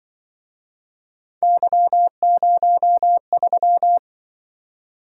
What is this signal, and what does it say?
Morse code, heard as a keyed tone, about 26 wpm, 700 Hz. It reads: Y03